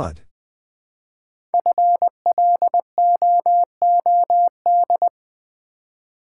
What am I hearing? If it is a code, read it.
Morse code, heard as a keyed tone, about 20 wpm, 700 Hz. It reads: FLOOD
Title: 5 Letter Words - First 3 Letters Given - 20wpm